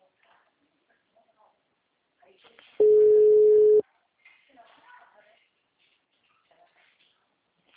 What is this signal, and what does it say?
Ringback tone (Europe, 425 Hz), with speech.